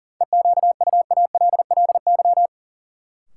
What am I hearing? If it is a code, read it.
Morse code, heard as a keyed tone, about 40 wpm, 700 Hz. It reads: EQUALLY
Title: Sentences from Top 1400 Words - New Word Drill - 40wpm